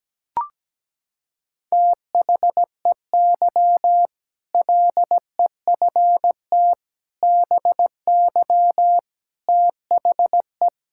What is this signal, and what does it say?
Morse code, heard as a keyed tone, about 17 wpm, 700 Hz. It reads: THEY LEFT BY THE
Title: Sentences from Top 1600 Words 17wpm